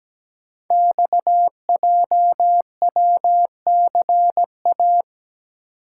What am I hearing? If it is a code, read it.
Morse code, heard as a keyed tone, about 17 wpm, 700 Hz. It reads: XJWCA
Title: Code Groups - 5 Letter:Numbers - 17wpm